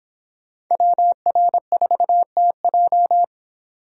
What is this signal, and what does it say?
Morse code, heard as a keyed tone, about 29 wpm, 700 Hz. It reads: WR4TJ